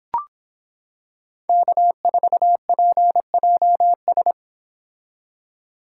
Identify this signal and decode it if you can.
Morse code, heard as a keyed tone, about 26 wpm, 700 Hz. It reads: K4PJS